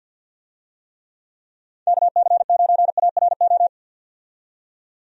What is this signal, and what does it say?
Morse code, heard as a keyed tone, about 52 wpm, 700 Hz. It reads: KC9ARO
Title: USA Call Signs - 6 Characters - 50wpm